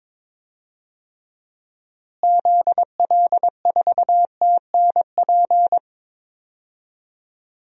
Morse code, heard as a keyed tone, about 22 wpm, 700 Hz. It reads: ZL4TNP
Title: DXCC-100 Call Signs - 6 Characters - Repeated 3x - 22wpm